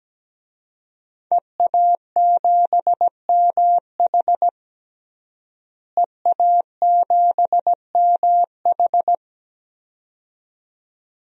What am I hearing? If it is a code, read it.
Morse code, heard as a keyed tone, about 17 wpm, 700 Hz. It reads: EA7MH EA7MH
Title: DXCC-100 Call Signs - 5 Characters - Repeated 2x - 17wpm